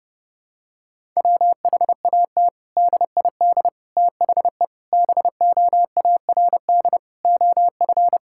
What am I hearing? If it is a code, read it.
Morse code, heard as a keyed tone, about 32 wpm, 700 Hz. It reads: WHAT DID THE BOARD OF
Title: Sentences from Top 1000 Words - Review Part 6 - 30wpm